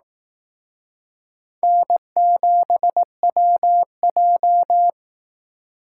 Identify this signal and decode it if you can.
Morse code, heard as a keyed tone, about 18 wpm, 700 Hz. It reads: N7WJ
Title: Speed Racing - USA Call Signs 15wpm